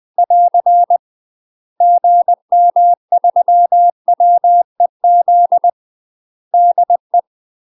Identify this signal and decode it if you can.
Morse code, heard as a keyed tone, about 20 wpm, 700 Hz. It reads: <AR> GM3WEZ DE
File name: UK MASTER QSOs TRAINING / 20WPM / UK MASTER QSOs 0067